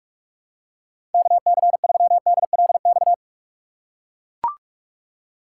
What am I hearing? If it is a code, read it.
Morse code, heard as a keyed tone, about 43 wpm, 700 Hz. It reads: KC3DLX
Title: USA Call Signs - 6 Characters - 8x Character Spacing - 45wpm